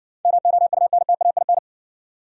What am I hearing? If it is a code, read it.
Morse code, heard as a keyed tone, about 58 wpm, 700 Hz. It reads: MOUNTAIN